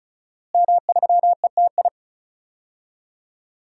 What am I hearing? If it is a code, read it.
Morse code, heard as a keyed tone, about 36 wpm, 700 Hz. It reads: M3ETI
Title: DXCC-100 Call Signs - 5 Characters - Repeated 3x - 35wpm